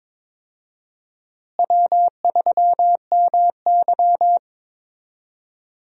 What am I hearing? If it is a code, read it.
Morse code, heard as a keyed tone, about 21 wpm, 700 Hz. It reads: W3MY
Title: USA Call Signs - 4 Characters - Repeated 3x - 22wpm